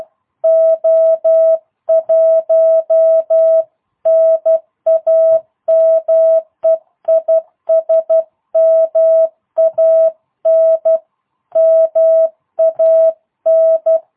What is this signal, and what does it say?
Morse code, heard as a keyed tone, about 12 wpm, 650 Hz. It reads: O1 NAME IS MAN MAN
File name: jh8cbh / 2022cw